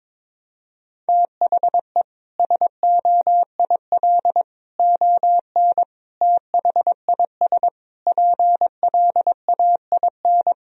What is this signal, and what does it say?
Morse code, heard as a keyed tone, about 22 wpm, 700 Hz. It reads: THE SOIL ON THIS PLAIN
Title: Sentences from Top 1000 Words - Review Part 4 - 22wpm